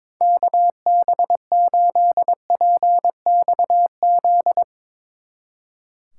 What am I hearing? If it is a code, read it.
Morse code, heard as a keyed tone, about 24 wpm, 700 Hz. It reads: KB8PXZ
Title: Speed Racing - USA 2x3 Call Signs 22wpm